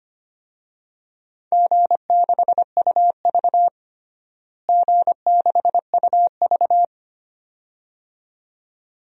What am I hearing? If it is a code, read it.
Morse code, heard as a keyed tone, about 25 wpm, 700 Hz. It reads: G6UV G6UV
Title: DXCC-100 Call Signs - 4 Characters - Repeated 2x - 25wpm